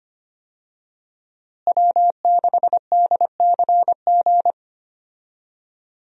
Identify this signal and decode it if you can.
Morse code, heard as a keyed tone, about 25 wpm, 700 Hz. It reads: W6DCG
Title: USA Call Signs - 5 Characters - Repeated 2x - 25wpm